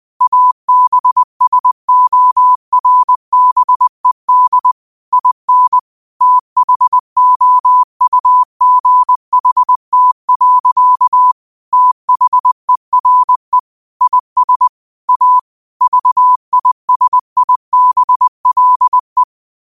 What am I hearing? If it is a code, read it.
Morse code, heard as a keyed tone, about 20 wpm, 1000 Hz. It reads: ABSORBED IN THOUGHT. THERE IS A VISIBLE
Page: New quotes every day in morse code at 20 Words per minute.